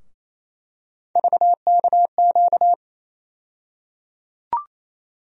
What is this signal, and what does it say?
Morse code, heard as a keyed tone, about 28 wpm, 700 Hz. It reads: VKQ
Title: USA Call Sign Postfixes - 3 Letters 28wpm